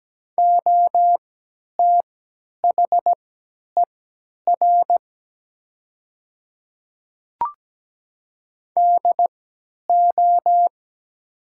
Morse code, heard as a keyed tone, about 17 wpm, spaced as for 10 wpm, 700 Hz. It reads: OTHER DO
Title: 3x Character Spacing - Top 300 Words - 17wpm